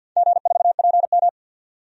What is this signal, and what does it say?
Morse code, heard as a keyed tone, about 48 wpm, 700 Hz. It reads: G4PM
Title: DXCC-100 Call Signs - 4 Characters - 50wpm